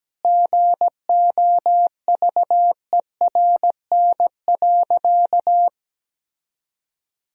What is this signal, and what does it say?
Morse code, heard as a keyed tone, about 17 wpm, 700 Hz. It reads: GOVERN.
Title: Sentences from Top 1000 Words - Review Part 2 - 17wpm